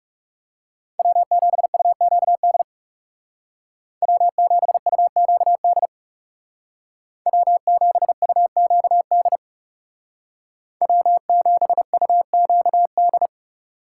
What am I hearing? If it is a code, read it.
Morse code, heard as a keyed tone, about 36 wpm, 700 Hz. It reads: W7UQD W7UQD W7UQD W7UQD
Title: Speed Racing - USA 1x3 Call Signs 30wpm